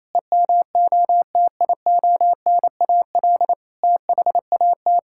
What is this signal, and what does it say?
Morse code, heard as a keyed tone, about 28 wpm, 700 Hz. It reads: EMOTIONAL THAT
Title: Sentences from Top 1000 Words - Review Part 5 - 28wpm